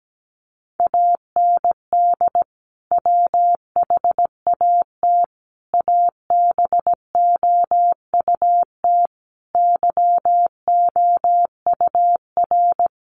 Morse code, heard as a keyed tone, about 17 wpm, 700 Hz. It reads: AND WHAT ABOUT YOUR
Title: Sentences from Top 200 Words 17wpm